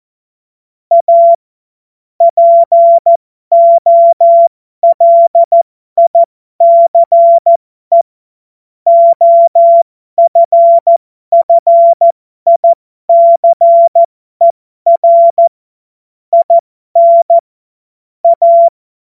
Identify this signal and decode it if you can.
Morse code, heard as a keyed tone, about 14 wpm, spaced as for 12 wpm, 680 Hz. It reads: A POLICE OFFICER IN A
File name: Morsekurs / Witze